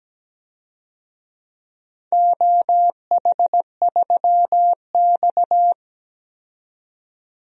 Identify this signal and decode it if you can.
Morse code, heard as a keyed tone, about 17 wpm, 700 Hz. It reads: OH3X
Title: DXCC-100 Call Signs - 4 Characters - 17wpm